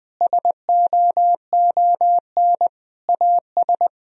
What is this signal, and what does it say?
Morse code, heard as a keyed tone, about 20 wpm, 700 Hz. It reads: SOON AS
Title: Sentences from Top 1000 Words - Review Part 2 - 20wpm